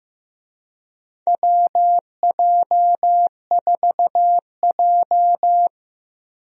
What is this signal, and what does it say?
Morse code, heard as a keyed tone, about 15 wpm, 700 Hz. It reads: WJ4J
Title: USA Call Signs - 4 Characters - 15wpm